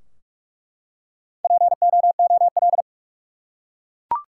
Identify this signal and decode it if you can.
Morse code, heard as a keyed tone, about 42 wpm, 700 Hz. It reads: POOL
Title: Four Letter Words - 45wpm